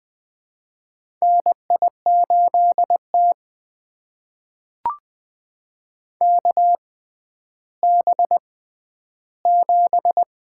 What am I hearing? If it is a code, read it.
Morse code, heard as a keyed tone, about 20 wpm, spaced as for 11 wpm, 700 Hz. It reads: NI8T KB7
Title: USA Call Signs - 4 Characters - 6x Character Spacing - 20wpm